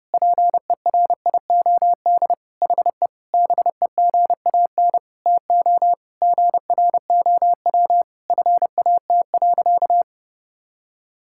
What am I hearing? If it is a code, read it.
Morse code, heard as a keyed tone, about 30 wpm, 700 Hz. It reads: PERIOD HE BEGAN TO GROW FAT.
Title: Sentences from Top 1000 Words - Review Part 5 - 30wpm